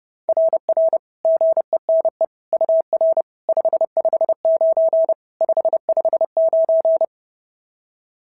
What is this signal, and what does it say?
Morse code, heard as a keyed tone, about 30 wpm, 690 Hz. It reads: RR GENE UR 559 559
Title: SOTA QSO Practice 30wpm